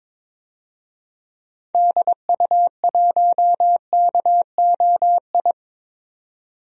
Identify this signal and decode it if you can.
Morse code, heard as a keyed tone, about 22 wpm, 700 Hz. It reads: DU1KOI
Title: DXCC-100 Call Signs - 6 Characters - 22wpm